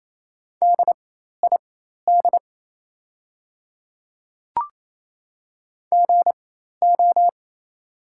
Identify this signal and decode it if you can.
Morse code, heard as a keyed tone, about 28 wpm, spaced as for 13 wpm, 700 Hz. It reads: DID GO
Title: 4x Character Spacing - Top 300 Words - 28wpm